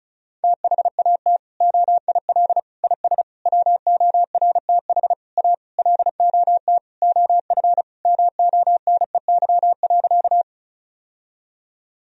Morse code, heard as a keyed tone, about 34 wpm, 700 Hz. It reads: THAT OIL IS WORTH A LOT OF MONEY.
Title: Sentences from Top 1000 Words - Review Part 4 - 35wpm